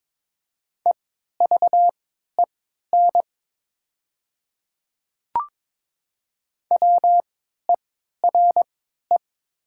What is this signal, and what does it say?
Morse code, heard as a keyed tone, about 22 wpm, spaced as for 12 wpm, 700 Hz. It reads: EVEN WERE
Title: 3x Character Spacing - Top 200 Words - 22wpm